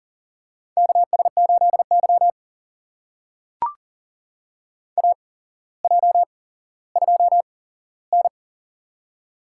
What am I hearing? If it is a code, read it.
Morse code, heard as a keyed tone, about 40 wpm, spaced as for 17 wpm, 700 Hz. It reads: KS8Y AJ2N
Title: USA Call Signs - 4 Characters - 8x Character Spacing - 40wpm